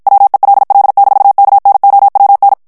Morse code, heard as a keyed tone, about 51 wpm, 800 Hz. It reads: WELL-KNOWN
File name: morse mp3 / ANGLAIS texte clair 1 vitesse 50 mots minute